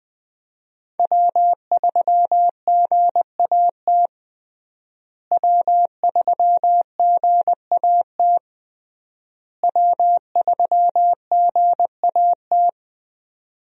Morse code, heard as a keyed tone, about 21 wpm, 700 Hz. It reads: W3GAT W3GAT W3GAT
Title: USA Call Signs - 5 Characters - Repeated 3x - 20wpm